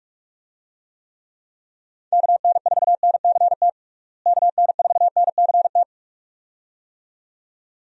Morse code, heard as a keyed tone, about 46 wpm, 700 Hz. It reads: KN4NCT KN4NCT
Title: USA Call Signs - 6 Characters - Repeated 2x - 45wpm